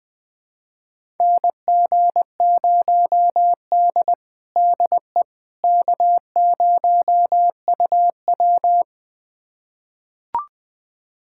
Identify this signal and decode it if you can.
Morse code, heard as a keyed tone, about 20 wpm, 700 Hz. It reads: NG0D DE K0UW
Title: USA Callsign DE Callsign - 4 Characters - 20wpm